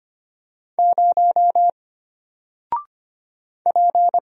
Morse code, heard as a keyed tone, about 25 wpm, 700 Hz. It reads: 0 P
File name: Single-Letter-Number-25wpm